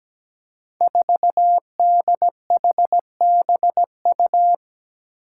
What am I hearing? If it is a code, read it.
Morse code, heard as a keyed tone, about 17 wpm, 700 Hz. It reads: 4DHBU
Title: Five Challenging Letter-Number Combinations 17wpm